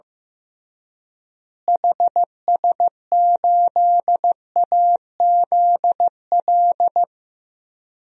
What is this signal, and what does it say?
Morse code, heard as a keyed tone, about 15 wpm, 700 Hz. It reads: HS8AZL
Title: DXCC-100 Call Signs - 6 Characters - Repeated 2x - 15wpm